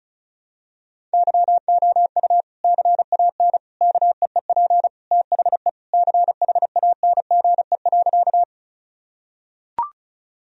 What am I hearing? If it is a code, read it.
Morse code, heard as a keyed tone, about 36 wpm, 700 Hz. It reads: YOU CAN KEEP THE CHANGE.
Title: Sentences from Top 1000 Words - Review Part 1 - 35wpm